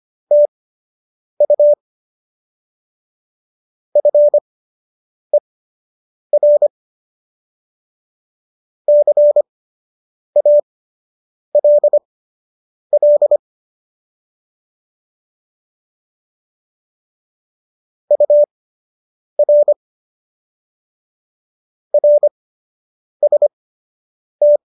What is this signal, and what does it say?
Morse code, heard as a keyed tone, about 25 wpm, spaced as for 8 wpm, 590 Hz. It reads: TU FER CALL UR RST